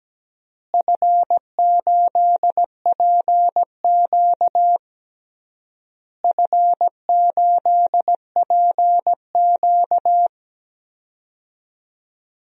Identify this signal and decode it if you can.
Morse code, heard as a keyed tone, about 17 wpm, 700 Hz. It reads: F8PQ F8PQ
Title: DXCC-100 Call Signs - 4 Characters - Repeated 3x - 17wpm